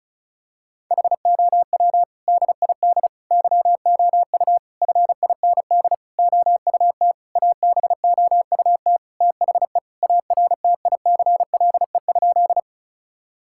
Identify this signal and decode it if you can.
Morse code, heard as a keyed tone, about 36 wpm, 700 Hz. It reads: HOW DID YOU FIND OUT ABOUT THE ARTICLE?
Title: Sentences from Top 1000 Words - Review Part 5 - 35wpm